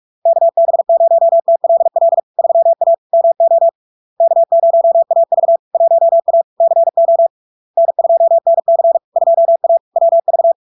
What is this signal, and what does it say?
Morse code, heard as a keyed tone, about 46 wpm, 670 Hz. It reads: KB0TLL 3A MO K0AV 1A CO N2NC 2A WV
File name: LARK-FD-2022-Log-in-CW-45wpm-670-Hz